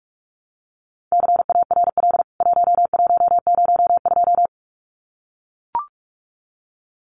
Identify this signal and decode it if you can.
Morse code, heard as a keyed tone, about 44 wpm, 700 Hz. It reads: CARL 1102
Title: CWT Exchange 45wpm